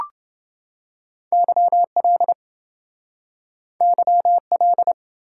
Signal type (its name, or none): Morse code